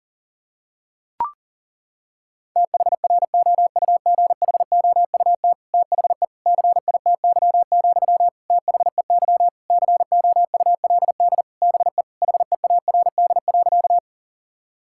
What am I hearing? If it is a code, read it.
Morse code, heard as a keyed tone, about 40 wpm, 700 Hz. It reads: THROUGHOUT THE CITY, THEY COULD BE HEARD.
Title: Sentences from Top 1000 Words - Review Part 5 - 40wpm